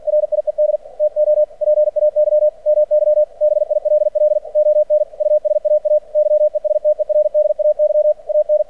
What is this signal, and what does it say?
Morse code, heard as a keyed tone, about 50 wpm, 600 Hz. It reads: RIED TO PAY MY BILL ON USAA YESTERDAY AN
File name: archived / CRSnet